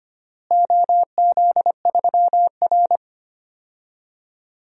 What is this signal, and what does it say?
Morse code, heard as a keyed tone, about 25 wpm, 700 Hz. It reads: OZ3R
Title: DXCC-100 Call Signs - 4 Characters - Repeated 3x - 25wpm